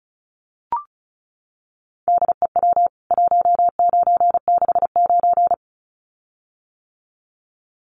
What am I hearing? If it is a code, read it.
Morse code, heard as a keyed tone, about 34 wpm, 700 Hz. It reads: DEW 1969
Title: CWT Exchange 35wpm